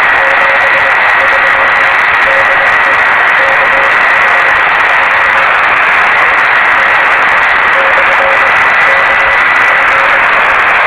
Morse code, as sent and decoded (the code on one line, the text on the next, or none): none